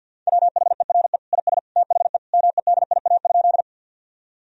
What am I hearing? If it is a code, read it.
Morse code, heard as a keyed tone, about 50 wpm, 700 Hz. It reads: WHERE IS THE MEDIA?
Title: Sentences from Top 1000 Words - Review Part 4 - 50wpm